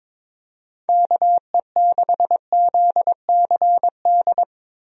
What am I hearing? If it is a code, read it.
Morse code, heard as a keyed tone, about 22 wpm, 700 Hz. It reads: KE6ZCD
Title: USA Call Signs - 6 Characters - Repeated 2x - 22wpm